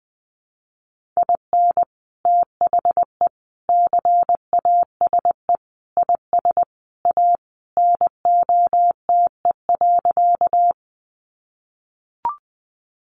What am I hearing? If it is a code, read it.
Morse code, heard as a keyed tone, about 20 wpm, 700 Hz. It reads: IN THE CASE IS A NOTE.
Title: Sentences from Top 700 Words 20wpm